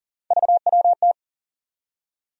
Morse code, heard as a keyed tone, about 40 wpm, 700 Hz. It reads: VWT
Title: USA Call Sign Postfixes - 3 Letters 40wpm